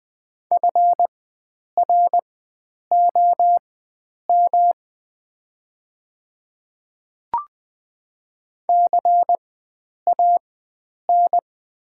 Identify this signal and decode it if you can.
Morse code, heard as a keyed tone, about 21 wpm, spaced as for 9 wpm, 700 Hz. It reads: FROM CAN
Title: 4x Character Spacing - Top 100 Words - 20wpm